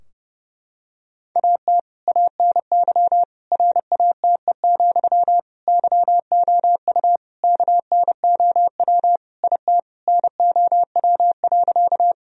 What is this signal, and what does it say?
Morse code, heard as a keyed tone, about 30 wpm, 700 Hz. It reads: AT ANY RATE, YOU KNOW IT NOW.
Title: Sentences from Top 1000 Words - Review Part 4 - 30wpm